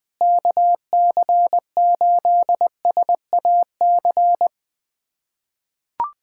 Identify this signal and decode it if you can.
Morse code, heard as a keyed tone, about 21 wpm, 700 Hz. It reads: KC8SAC
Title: USA Callsign DE Callsign 20wpm